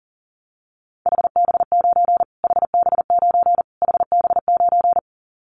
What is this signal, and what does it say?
Morse code, heard as a keyed tone, about 39 wpm, 700 Hz. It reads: HB9 HB9 HB9
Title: Callsign Prefixes for Top 25 DXCC - Repeated 3x - 40wpm